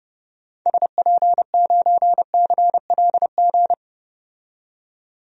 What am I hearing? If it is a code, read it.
Morse code, heard as a keyed tone, about 32 wpm, 700 Hz. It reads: SP9CLG
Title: DXCC-100 Call Signs - 6 Characters - 30wpm